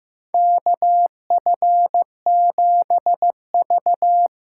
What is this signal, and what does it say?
Morse code, heard as a keyed tone, about 15 wpm, 700 Hz. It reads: KF7V
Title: USA Call Signs - 6 Characters - 8x Character Spacing - 15wpm